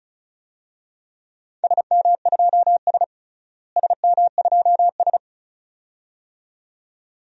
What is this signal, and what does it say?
Morse code, heard as a keyed tone, about 34 wpm, 700 Hz. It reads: SM2S SM2S
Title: DXCC-100 Call Signs - 4 Characters - Repeated 2x - 35wpm